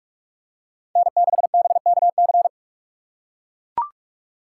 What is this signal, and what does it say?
Morse code, heard as a keyed tone, about 43 wpm, 700 Hz. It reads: N6BKC